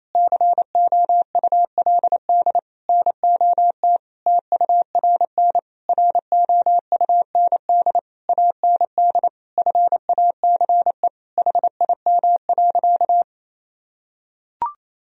Morse code, heard as a keyed tone, about 28 wpm, 700 Hz. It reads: COULD NOT TURN ROUND AND FACE HIM.
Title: Sentences from Top 1000 Words - Review Part 2 - 28wpm